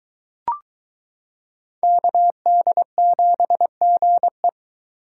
Morse code, heard as a keyed tone, about 22 wpm, 700 Hz. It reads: KD7GE